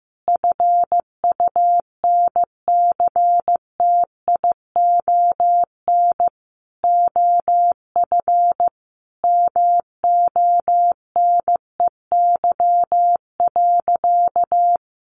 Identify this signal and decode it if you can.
Morse code, heard as a keyed tone, about 15 wpm, 700 Hz. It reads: FUNCTION OF MONEY.
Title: Sentences from Top 1100 Words 15wpm